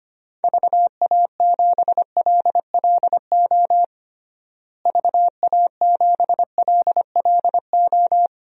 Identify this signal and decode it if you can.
Morse code, heard as a keyed tone, about 25 wpm, 700 Hz. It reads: VA7LLO VA7LLO
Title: DXCC-100 Call Signs - 6 Characters - Repeated 2x - 25wpm